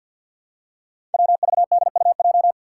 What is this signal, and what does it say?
Morse code, heard as a keyed tone, about 50 wpm, 700 Hz. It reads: WVDUJ
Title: Five Challenging Letter-Number Combinations 50wpm